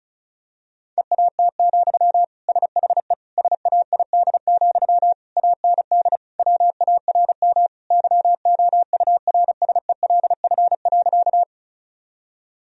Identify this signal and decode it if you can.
Morse code, heard as a keyed tone, about 34 wpm, 700 Hz. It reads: EAT, SHE SAID, AND WARM YOURSELF.
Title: Sentences from Top 1000 Words - Review Part 2 - 35wpm